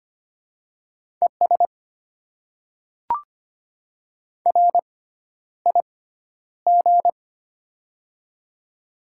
Morse code, heard as a keyed tone, about 25 wpm, spaced as for 11 wpm, 700 Hz. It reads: ES RIG
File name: Top 20 QSO Elements - 6x Character Spacing - 25wpm